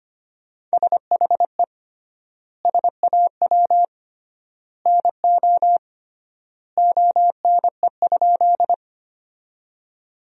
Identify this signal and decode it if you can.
Morse code, heard as a keyed tone, about 25 wpm, 700 Hz. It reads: SHE SAW NO ONE?
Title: 3x Word-Spacing - Sentences from Top 300 Words - 25wpm